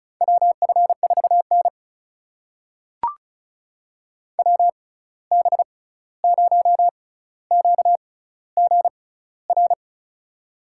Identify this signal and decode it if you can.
Morse code, heard as a keyed tone, about 36 wpm, spaced as for 16 wpm, 700 Hz. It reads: WF4N WB0QGR
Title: USA Call Signs - 6x Character Spacing - 35wpm